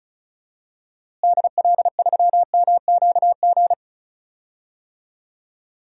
Morse code, heard as a keyed tone, about 36 wpm, 700 Hz. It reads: DL3MQG
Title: DXCC-100 Call Signs - 6 Characters - Repeated 3x - 35wpm